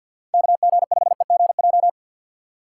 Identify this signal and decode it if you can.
Morse code, heard as a keyed tone, about 53 wpm, 700 Hz. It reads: KG5EGJ
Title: USA Call Signs - 3x Character Spacing - 50wpm